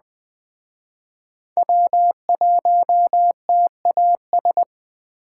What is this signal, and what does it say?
Morse code, heard as a keyed tone, about 20 wpm, 700 Hz. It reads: W1TAS